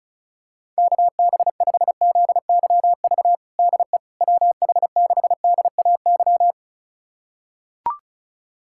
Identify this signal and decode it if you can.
Morse code, heard as a keyed tone, about 36 wpm, 690 Hz. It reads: KB5ZYV DE WH6DAY
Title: USA Callsign DE Callsign - 6 Characters - 35wpm